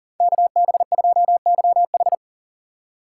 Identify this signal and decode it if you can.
Morse code, heard as a keyed tone, about 50 wpm, 700 Hz. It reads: KB2YH